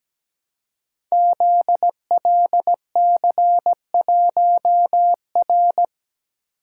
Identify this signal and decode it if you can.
Morse code, heard as a keyed tone, about 17 wpm, 700 Hz. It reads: ZLC1R
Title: Code Groups - 5 Letter:Numbers - 17wpm